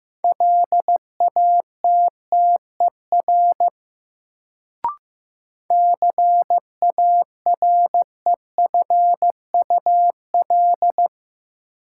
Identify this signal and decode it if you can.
Morse code, heard as a keyed tone, about 15 wpm, 700 Hz. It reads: LATTER CAREFUL
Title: Sentences from Top 1600 Words - New Word Drill - 15wpm